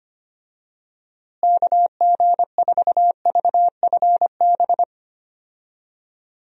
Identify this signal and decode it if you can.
Morse code, heard as a keyed tone, about 25 wpm, 700 Hz. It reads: KG4VFB
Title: USA Call Signs - 6 Characters - Repeated 2x - 25wpm